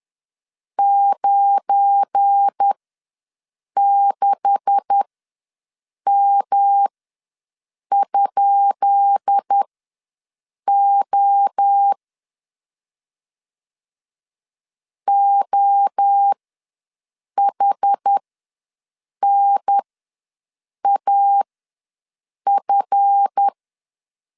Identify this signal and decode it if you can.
Morse code, heard as a keyed tone, about 10 wpm, spaced as for 6 wpm, 800 Hz. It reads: 96M?O OHNAF
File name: LEZIONI / 10 SIGNS / 10B INT ws 10wpm 4134